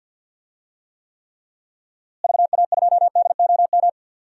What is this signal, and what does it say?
Morse code, heard as a keyed tone, about 46 wpm, 700 Hz. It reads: VA2DOM
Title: DXCC-100 Call Signs - 6 Characters - Repeated 3x - 50wpm